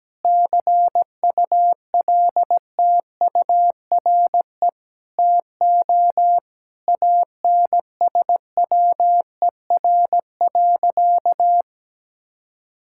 Morse code, heard as a keyed tone, about 17 wpm, 700 Hz. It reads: CULTURE TO ANSWER.
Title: Sentences from Top 1000 Words - Review Part 6 - 17wpm